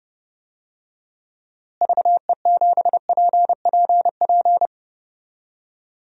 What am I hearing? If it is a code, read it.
Morse code, heard as a keyed tone, about 32 wpm, 700 Hz. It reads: VE7PPP
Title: DXCC-100 Call Signs - 6 Characters - Repeated 3x - 30wpm